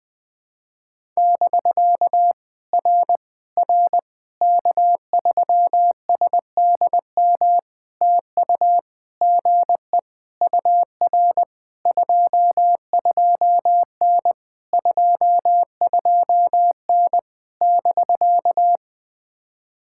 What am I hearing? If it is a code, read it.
Morse code, heard as a keyed tone, about 20 wpm, 700 Hz. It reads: <BK> R R K3SDM TU GE UR 22N 22N <BK>
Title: POTA QSO Practice - 20wpm